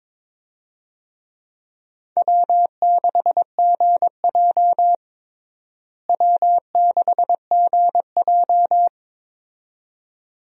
Morse code, heard as a keyed tone, about 22 wpm, 700 Hz. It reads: W6GJ W6GJ
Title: USA Call Signs - 4 Characters - Repeated 2x - 22wpm